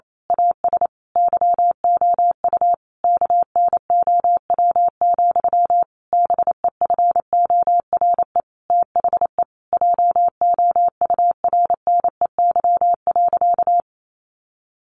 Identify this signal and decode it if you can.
Morse code, heard as a keyed tone, about 28 wpm, 700 Hz. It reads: AS YOU KNOW, BEFORE THE JOURNEY.
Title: A Sound Of Thunder 28wpm